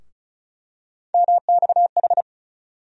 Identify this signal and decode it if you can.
Morse code, heard as a keyed tone, about 34 wpm, 700 Hz. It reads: MXH
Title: USA Call Sign Postfixes - 3 Letters 35wpm